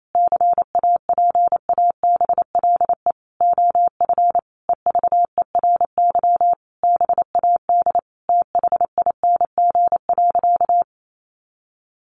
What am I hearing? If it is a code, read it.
Morse code, heard as a keyed tone, about 28 wpm, 700 Hz. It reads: CAPABLE OF EVERY BAD THING.
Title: Sentences from Top 1200 Words 28wpm